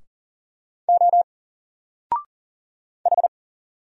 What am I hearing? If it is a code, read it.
Morse code, heard as a keyed tone, about 40 wpm, 700 Hz. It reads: O H
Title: Single Letter-Number Alt 40wpm